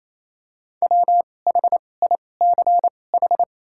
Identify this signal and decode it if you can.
Morse code, heard as a keyed tone, about 28 wpm, spaced as for 20 wpm, 700 Hz. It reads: WHICH
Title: 2x Character Spacing - Top 300 Words - 28wpm